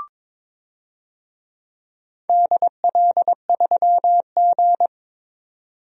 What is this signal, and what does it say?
Morse code, heard as a keyed tone, about 22 wpm, 700 Hz. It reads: DL3G
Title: DXCC-100 Call Signs - 4 Characters - 22wpm